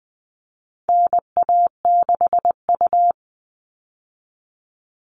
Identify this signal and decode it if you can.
Morse code, heard as a keyed tone, about 20 wpm, 700 Hz. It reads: NA6U
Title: All Cut Numbers - 4-Digit 20wpm